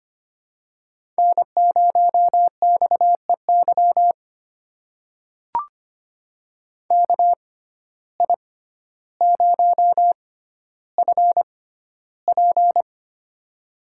Morse code, heard as a keyed tone, about 25 wpm, spaced as for 13 wpm, 700 Hz. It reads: N0XEY KI0FP
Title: USA Call Signs - 5 Characters - 6x Character Spacing - 25wpm